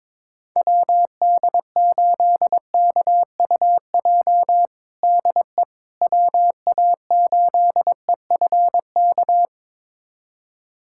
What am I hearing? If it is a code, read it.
Morse code, heard as a keyed tone, about 21 wpm, 700 Hz. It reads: WD8KUJ DE WA8EFK
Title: USA Callsign DE Callsign 22wpm